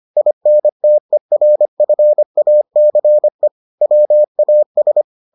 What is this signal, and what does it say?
Morse code, heard as a keyed tone, about 25 wpm, 600 Hz. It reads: INTERFACE WAS